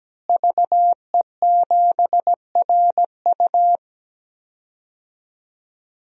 Morse code, heard as a keyed tone, about 17 wpm, 700 Hz. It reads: VE7RU
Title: DXCC-100 Call Signs - 5 Characters - Repeated 2x - 17wpm